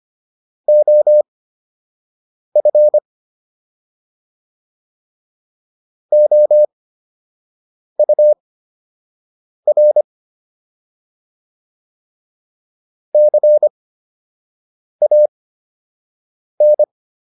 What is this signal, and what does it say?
Morse code, heard as a keyed tone, about 25 wpm, spaced as for 6 wpm, 600 Hz. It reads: OF OUR CAN